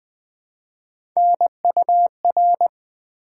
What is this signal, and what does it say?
Morse code, heard as a keyed tone, about 20 wpm, 700 Hz. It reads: NUR
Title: Build A Word - 1 Letter At A Time - 3 to 5-Letter Words - 20wpm